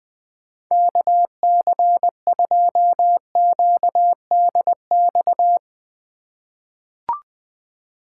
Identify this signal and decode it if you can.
Morse code, heard as a keyed tone, about 20 wpm, 700 Hz. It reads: KC2QDX
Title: USA Call Signs - 6 Characters - 4x Character Spacing - 20wpm